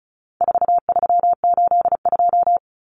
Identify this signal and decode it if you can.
Morse code, heard as a keyed tone, about 34 wpm, 700 Hz. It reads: 4382
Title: Common Cut Numbers - 4-Digit 35wpm